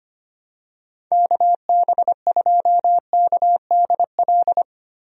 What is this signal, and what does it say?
Morse code, heard as a keyed tone, about 25 wpm, 700 Hz. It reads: KB2KDL
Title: USA Call Signs - 6 Characters - 25wpm